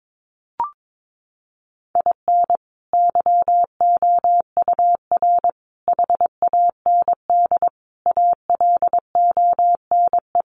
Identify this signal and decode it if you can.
Morse code, heard as a keyed tone, about 22 wpm, 700 Hz. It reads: IN YOUR HAND ALONE
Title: Sentences from Top 1400 Words 22wpm